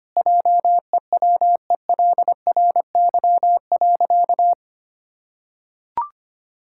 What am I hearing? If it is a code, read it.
Morse code, heard as a keyed tone, about 25 wpm, 700 Hz. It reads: JEWELRY.
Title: Pangram Challenge v2 - 25wpm